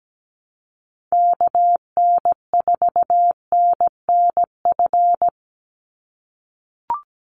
Morse code, heard as a keyed tone, about 17 wpm, 700 Hz. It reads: KN4NNF
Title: USA Call Signs 17wpm